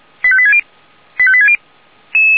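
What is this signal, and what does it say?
Five-tone selective call (ZVEI), heard as quick runs of five tones, tones of about 70 ms. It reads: 85860 85860
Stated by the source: ZVEI five-tone selective call received on an SDR.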